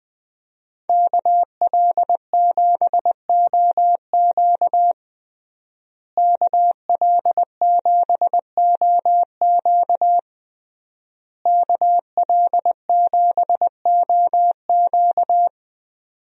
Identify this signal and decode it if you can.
Morse code, heard as a keyed tone, about 20 wpm, 700 Hz. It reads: KL7OQ KL7OQ KL7OQ
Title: DXCC-100 Call Signs - 5 Characters - Repeated 3x - 20wpm